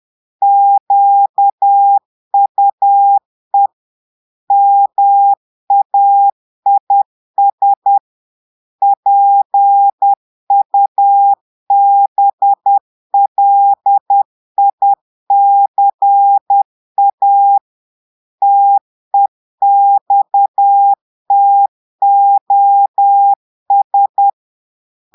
Morse code, heard as a keyed tone, about 10 wpm, 800 Hz. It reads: QUE MAIS PUBLICA TEXTOS
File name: Portugais - texte en clair 1 - vitesse - 10 mots minute